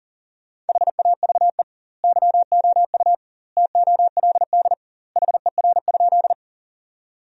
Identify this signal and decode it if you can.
Morse code, heard as a keyed tone, about 40 wpm, 700 Hz. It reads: HAVE YOU TOLD HER?
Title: 2x Word-Spacing - Sentences from Top 400 Words - 40wpm